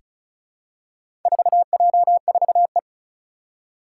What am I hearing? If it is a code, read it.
Morse code, heard as a keyed tone, about 36 wpm, 700 Hz. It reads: 4J4E
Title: Four Challenging Letter-Number Combinations 35wpm